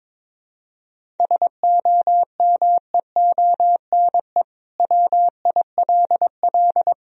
Morse code, heard as a keyed tone, about 22 wpm, 700 Hz. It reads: SOMEONE WILL
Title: Sentences from Top 1000 Words - Review Part 1 - 22wpm